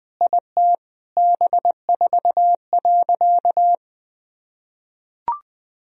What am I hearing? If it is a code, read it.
Morse code, heard as a keyed tone, about 20 wpm, 700 Hz. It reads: IT B4.
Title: Sentences from Top 400 Words and Abbreviations 20wpm